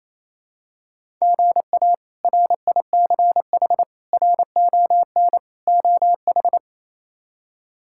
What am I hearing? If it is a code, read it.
Morse code, heard as a keyed tone, about 28 wpm, 700 Hz. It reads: GA RICH RON OH
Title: K1USN SST QSO Practice - 28wpm